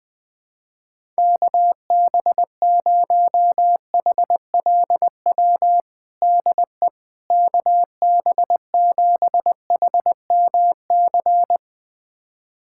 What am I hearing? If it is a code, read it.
Morse code, heard as a keyed tone, about 20 wpm, 700 Hz. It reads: KB0HLW DE KB7HMC
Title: USA Callsign DE Callsign - 6 Characters - 20wpm